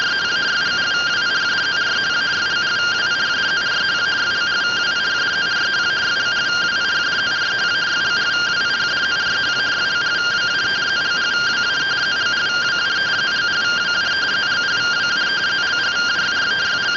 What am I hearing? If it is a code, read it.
Two-tone teleprinter signal, about 170 Hz shift.
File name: findings / MR AXIAL 0mm 6mm.gen-38 rtty rx